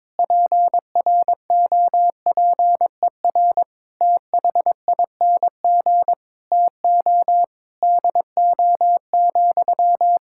Morse code, heard as a keyed tone, about 22 wpm, 700 Hz. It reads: PROPER THING TO DO,
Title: Sentences from Top 1000 Words - Review Part 5 - 22wpm